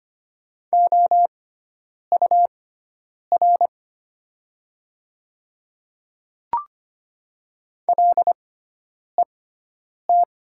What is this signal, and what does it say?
Morse code, heard as a keyed tone, about 25 wpm, spaced as for 9 wpm, 700 Hz. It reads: OUR LET
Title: 6x Character Spacing - Top 300 Words - 25wpm